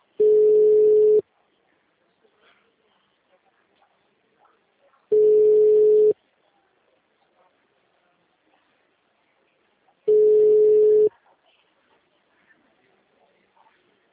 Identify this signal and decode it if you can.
Ringback tone (Europe, 425 Hz).
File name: out- -2019- - - .64236